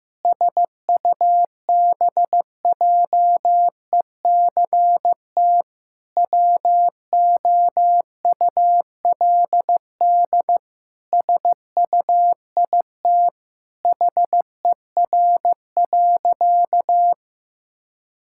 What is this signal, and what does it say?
Morse code, heard as a keyed tone, about 15 wpm, 700 Hz. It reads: SUBJECT WOULD SUIT HER.
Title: Sentences from Top 1600 Words 15wpm